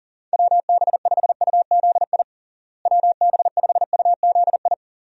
Morse code, heard as a keyed tone, about 40 wpm, 700 Hz. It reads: WB5UZI WB5UZI
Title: USA Call Signs - 6 Characters - Repeated 3x - 40wpm